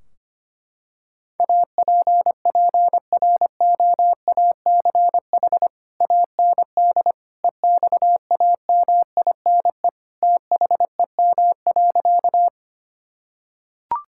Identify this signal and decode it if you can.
Morse code, heard as a keyed tone, about 25 wpm, 700 Hz. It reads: APPROACH AND EXAMINE THEM.
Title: Sentences from Top 1600 Words 25wpm